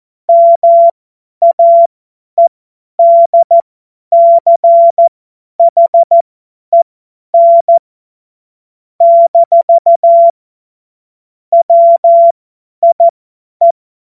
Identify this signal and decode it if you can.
Morse code, heard as a keyed tone, about 14 wpm, spaced as for 10 wpm, 680 Hz. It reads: MAEDCHEN - WIE
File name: Morsekurs / Witze / Witz 26. Oktober 2021, 50 F70